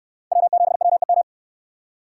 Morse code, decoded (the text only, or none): W7WEM